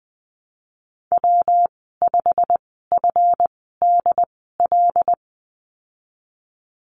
Morse code, heard as a keyed tone, about 20 wpm, spaced as for 15 wpm, 700 Hz. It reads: W5FDL
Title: 167 Lesson 27g - Number 7 - Call Signs 20wpm